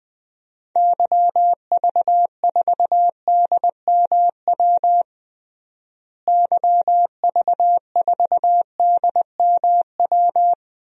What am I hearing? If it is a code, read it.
Morse code, heard as a keyed tone, about 20 wpm, 700 Hz. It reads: YV4DMW YV4DMW
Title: DXCC-100 Call Signs - 6 Characters - Repeated 2x - 20wpm